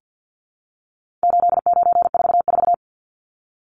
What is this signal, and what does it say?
Morse code, heard as a keyed tone, about 50 wpm, 700 Hz. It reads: 8944
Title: Numbers - 4-Digit 50wpm